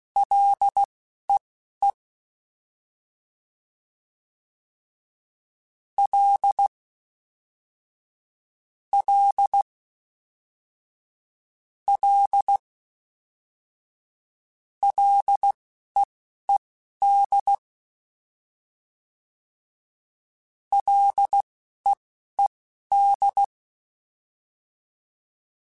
Morse code, heard as a keyed tone, about 16 wpm, spaced as for 11 wpm, 800 Hz. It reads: LEE L L L LEED LEED
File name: F and L / F and L mp3 / F and L Prefixes 3x2 Twice @10